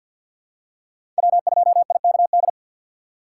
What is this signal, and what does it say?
Morse code, heard as a keyed tone, about 52 wpm, 700 Hz. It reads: W2IKD
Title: USA Call Signs - 5 Characters - 3x Character Spacing - 50wpm